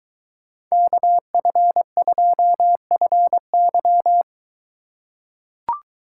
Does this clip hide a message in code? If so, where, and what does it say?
Yes: Morse code from 0.7 to 4.2 s, heard as a keyed tone, about 23 wpm, 700 Hz. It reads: KF2FY